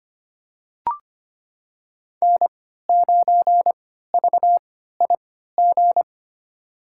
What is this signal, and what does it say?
Morse code, heard as a keyed tone, about 25 wpm, spaced as for 14 wpm, 700 Hz. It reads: N9VIG
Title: USA Call Signs - 5 Characters - 3x Character Spacing - 25wpm